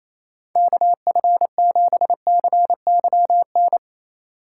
Morse code, heard as a keyed tone, about 28 wpm, 700 Hz. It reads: KF7CYN
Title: USA Call Signs - 6 Characters - 6x Character Spacing - 28wpm